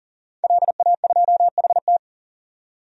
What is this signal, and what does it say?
Morse code, heard as a keyed tone, about 40 wpm, 700 Hz. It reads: LA2HT
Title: DXCC-100 Call Signs - 5 Characters - 40wpm